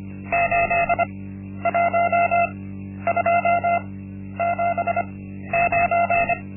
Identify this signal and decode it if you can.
Morse code, heard as a keyed tone, about 25 wpm, spaced as for 11 wpm, 690 Hz. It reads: 81279
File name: Linksmakalnis-1